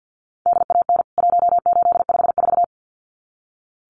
Morse code, heard as a keyed tone, about 48 wpm, 700 Hz. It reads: DAN 1854
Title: CWT Exchange 50wpm